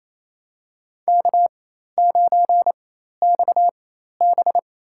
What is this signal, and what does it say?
Morse code, heard as a keyed tone, about 28 wpm, spaced as for 13 wpm, 700 Hz. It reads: K9XB
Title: USA Call Signs - 4 Characters - 4x Character Spacing - 28wpm